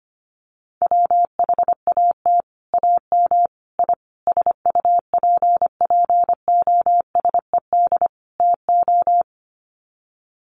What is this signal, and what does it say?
Morse code, heard as a keyed tone, about 25 wpm, 700 Hz. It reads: WHAT AM I SUPPOSED TO
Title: Sets of 5 Words Encore2 25wpm